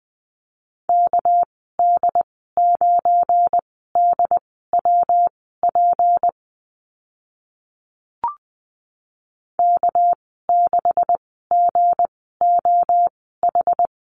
Morse code, heard as a keyed tone, about 20 wpm, spaced as for 14 wpm, 700 Hz. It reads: KD9DWP K6GOH
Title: 245 Lesson 39g - Letter z - Call Signs 20wpm